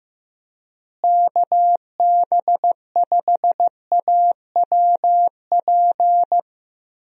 Morse code, heard as a keyed tone, about 15 wpm, 700 Hz. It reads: KB5AWP
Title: USA Call Signs - 6 Characters - Repeated 2x - 15wpm